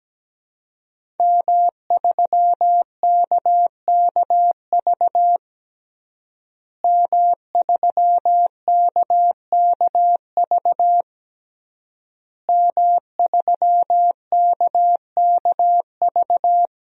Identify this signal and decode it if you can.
Morse code, heard as a keyed tone, about 17 wpm, 700 Hz. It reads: M3KKV M3KKV M3KKV
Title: DXCC-100 Call Signs - 5 Characters - Repeated 3x - 17wpm